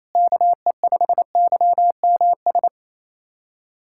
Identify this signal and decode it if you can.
Morse code, heard as a keyed tone, about 28 wpm, 700 Hz. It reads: KE5YMS
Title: USA Callsign DE Callsign - 6 Characters - 28wpm